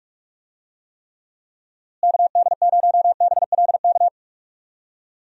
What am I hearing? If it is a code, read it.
Morse code, heard as a keyed tone, about 46 wpm, 700 Hz. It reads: KD0BLK
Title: USA Call Signs - 6 Characters - Repeated 3x - 45wpm